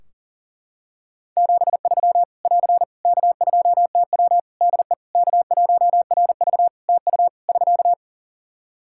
Morse code, heard as a keyed tone, about 40 wpm, 700 Hz. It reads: 73 <AR> K2TW DE K1RV TU <SK>
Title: K1USN SST QSO Practice 40wpm